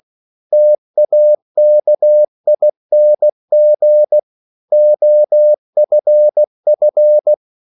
Morse code, heard as a keyed tone, about 16 wpm, 600 Hz. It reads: TAKING OFF